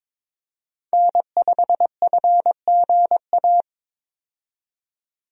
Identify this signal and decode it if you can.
Morse code, heard as a keyed tone, about 21 wpm, 700 Hz. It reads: N5FGA
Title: USA Call Signs - 5 Characters - 22wpm